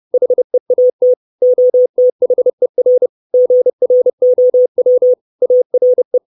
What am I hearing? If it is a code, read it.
Morse code, heard as a keyed tone, about 30 wpm, 500 Hz. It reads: HEAT OTHER GROW ARE
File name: 632 Words 30WPM